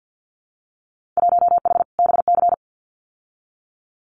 Morse code, heard as a keyed tone, about 50 wpm, 700 Hz. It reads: 1H BC
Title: Winter Field Day Exchanges Encore 50wpm